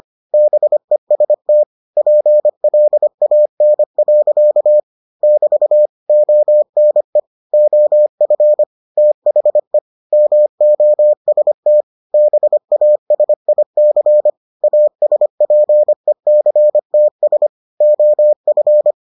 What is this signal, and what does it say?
Morse code, heard as a keyed tone, about 25 wpm, 600 Hz. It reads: BEST PLAN. <BT> ONE OF THE MOST BASIC ASPECTS OF